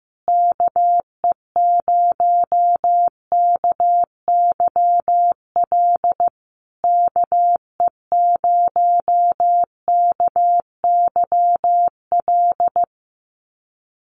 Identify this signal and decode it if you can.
Morse code, heard as a keyed tone, about 15 wpm, 700 Hz. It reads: KE0KYL KE0KYL
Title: USA Call Signs Repeated 3x 15wpm